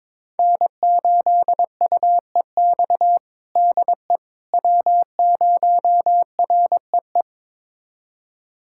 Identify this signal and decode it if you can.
Morse code, heard as a keyed tone, about 22 wpm, 700 Hz. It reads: N8UEX DE W0REE
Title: USA Callsign DE Callsign - 5 Characters - 22wpm